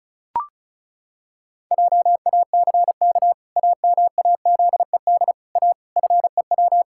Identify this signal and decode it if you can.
Morse code, heard as a keyed tone, about 36 wpm, 700 Hz. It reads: JACK AMAZED A FEW
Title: Pangram Challenge v2 - 35wpm